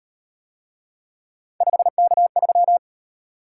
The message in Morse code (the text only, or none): 5K3